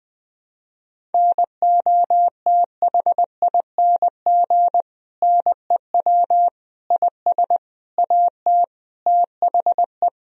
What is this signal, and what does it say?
Morse code, heard as a keyed tone, about 20 wpm, 700 Hz. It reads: NOTHING NEW IS AT THE
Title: Sentences from Top 1000 Words - Review Part 1 - 20wpm